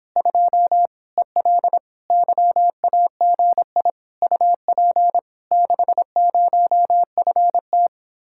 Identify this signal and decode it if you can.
Morse code, heard as a keyed tone, about 26 wpm, 700 Hz. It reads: 2 EL YAGI UP 60FT